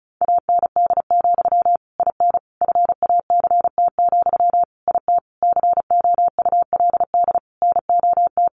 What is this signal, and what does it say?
Morse code, heard as a keyed tone, about 34 wpm, 700 Hz. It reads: AND, IN FACT, IT COULD NOT
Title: Sentences from Top 1200 Words 35wpm